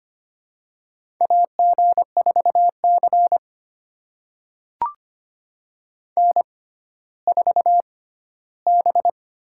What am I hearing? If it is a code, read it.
Morse code, heard as a keyed tone, about 25 wpm, spaced as for 14 wpm, 700 Hz. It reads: AG4C N4B
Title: USA Call Signs - 4 Characters - 6x Character Spacing - 25wpm